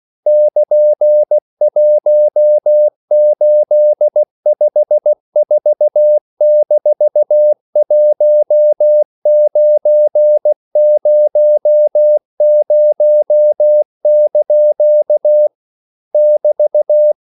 Morse code, heard as a keyed tone, about 16 wpm, 600 Hz. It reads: <KN>1854-1900) <BT>